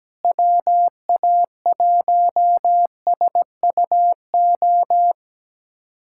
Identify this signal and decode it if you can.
Morse code, heard as a keyed tone, about 17 wpm, 700 Hz. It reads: WA1SUO
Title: USA Callsign DE Callsign - 6 Characters - 17wpm